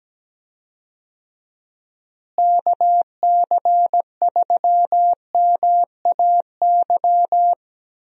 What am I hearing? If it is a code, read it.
Morse code, heard as a keyed tone, about 17 wpm, 700 Hz. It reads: KC3MAY
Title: USA Call Signs - 6 Characters - Repeated 3x - 17wpm